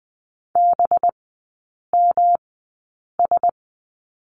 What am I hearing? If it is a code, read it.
Morse code, heard as a keyed tone, about 20 wpm, spaced as for 8 wpm, 700 Hz. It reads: BMS
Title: 205 Lesson 33e - Letter j - Three Characters 20wpm